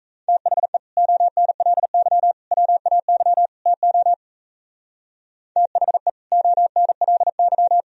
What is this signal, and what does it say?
Morse code, heard as a keyed tone, about 42 wpm, 700 Hz. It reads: THE ONLY WAY TO THE ONLY